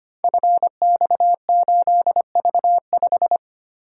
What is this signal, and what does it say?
Morse code, heard as a keyed tone, about 25 wpm, 700 Hz. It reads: FX8V5
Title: Code Groups - 5 Letter:Numbers - 25wpm